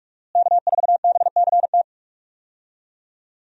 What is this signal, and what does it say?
Morse code, heard as a keyed tone, about 43 wpm, 700 Hz. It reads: K4BCT
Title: USA Call Signs - 5 Characters - Repeated 3x - 45wpm